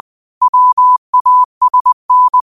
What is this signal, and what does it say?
Morse code, heard as a keyed tone, about 20 wpm, 1000 Hz. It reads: WASN
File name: qotdfiles / 20wpm